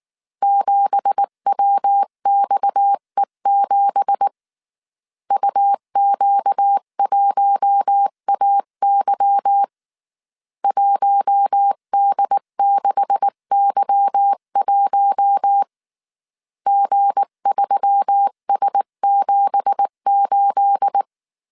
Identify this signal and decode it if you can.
Morse code, heard as a keyed tone, about 19 wpm, 800 Hz. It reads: 7WXE7 UQ1AY 1D6Y1 G3S78